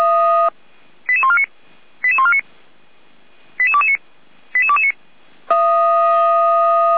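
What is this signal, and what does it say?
Five-tone selective call (ZVEI), heard as quick runs of five tones, tones of about 70 ms. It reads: 88149 88149 88229 88229
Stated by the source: ZVEI five-tone selective call 88329 received via SDR.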